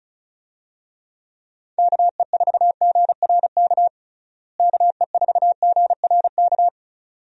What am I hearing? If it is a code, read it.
Morse code, heard as a keyed tone, about 36 wpm, 700 Hz. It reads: KE4GRK KE4GRK
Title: USA Call Signs - 6 Characters - Repeated 2x - 35wpm